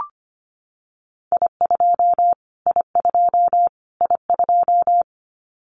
Morse code, heard as a keyed tone, about 25 wpm, 700 Hz. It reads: I2 I2 I2
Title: Callsign Prefixes for Top 25 DXCC - Repeated 3x - 25wpm